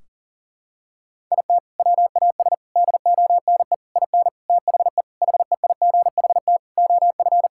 Morse code, heard as a keyed tone, about 40 wpm, 700 Hz. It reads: IT WAS DONE IN THE HEIGHT OF
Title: Sentences from Top 1600 Words 40wpm